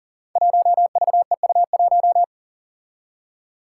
Morse code, heard as a keyed tone, about 40 wpm, 700 Hz. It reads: 1VEU1
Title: Five Challenging Letter-Number Combinations 40wpm